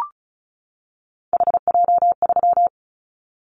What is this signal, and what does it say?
Morse code, heard as a keyed tone, about 36 wpm, 700 Hz. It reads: HJ3
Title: Callsign Prefixes for Top 100 DXCC - 35wpm